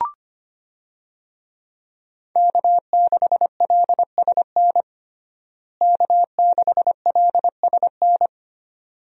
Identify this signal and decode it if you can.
Morse code, heard as a keyed tone, about 25 wpm, 700 Hz. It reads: K6LSN K6LSN
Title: USA Call Signs - 5 Characters - Repeated 2x - 25wpm